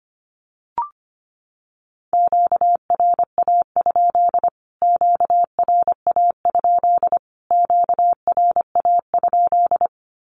Morse code, heard as a keyed tone, about 25 wpm, 700 Hz. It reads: QRA? QRA? QRA?
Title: Q-Codes Repeated 3x 25wpm